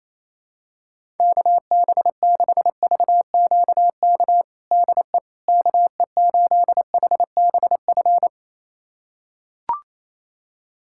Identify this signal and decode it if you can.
Morse code, heard as a keyed tone, about 28 wpm, 700 Hz. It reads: KB6VQK DE KE8HBF
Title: USA Callsign DE Callsign - 6 Characters - 28wpm